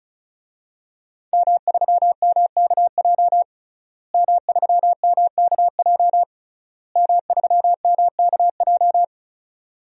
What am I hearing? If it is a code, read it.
Morse code, heard as a keyed tone, about 36 wpm, 700 Hz. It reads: M3MKJ M3MKJ M3MKJ
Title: DXCC-100 Call Signs - 5 Characters - Repeated 3x - 35wpm